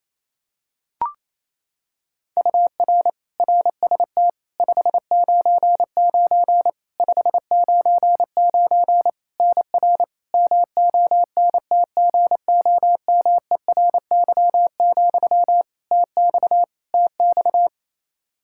Morse code, heard as a keyed tone, about 28 wpm, 700 Hz. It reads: UR RST 599 599 NR MONTGOMERY, TX TX
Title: Practice QSO - Exchange 1 - Beginning - 28wpm